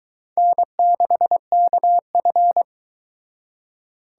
Morse code, heard as a keyed tone, about 22 wpm, 700 Hz. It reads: N6KF